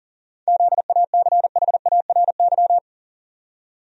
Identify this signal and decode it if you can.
Morse code, heard as a keyed tone, about 40 wpm, 700 Hz. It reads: ZACHARY
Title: USA Common Names - 40wpm